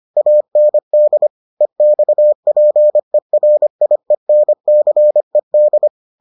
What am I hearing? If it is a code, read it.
Morse code, heard as a keyed tone, about 25 wpm, 600 Hz. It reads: AND EXPERIENCED